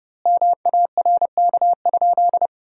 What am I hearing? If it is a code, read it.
Morse code, heard as a keyed tone, about 32 wpm, 700 Hz. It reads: MARK?
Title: 3x Word-Spacing - Sentences from Top 300 Words - 30wpm